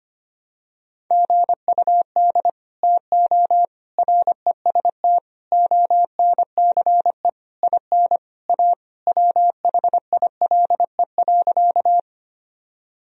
Morse code, heard as a keyed tone, about 25 wpm, 700 Hz. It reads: GUD TO REST ONCE IN A WHILE.
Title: Sentences from Top 500 Words and Abbreviations 25wpm